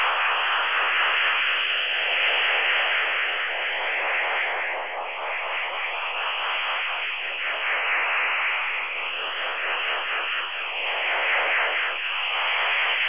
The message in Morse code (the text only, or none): none